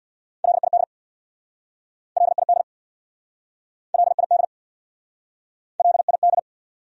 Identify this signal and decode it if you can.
Morse code, heard as a keyed tone, about 66 wpm, 700 Hz. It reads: LID LID LID LID